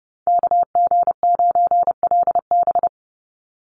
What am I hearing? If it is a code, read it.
Morse code, heard as a keyed tone, about 30 wpm, 700 Hz. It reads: KG9LB
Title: Call Signs - 30wpm